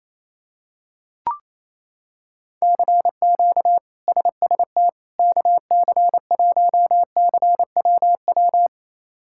Morse code, heard as a keyed tone, about 28 wpm, 700 Hz. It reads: CQ SST KC1CWW
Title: K1USN SST QSO Practice - 28wpm